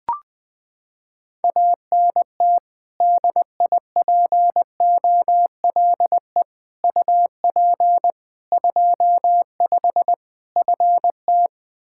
Morse code, heard as a keyed tone, about 21 wpm, 700 Hz. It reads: ANT DIPOLE UP 25 FT
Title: Practice QSO - Exchange 2 - Info - 20wpm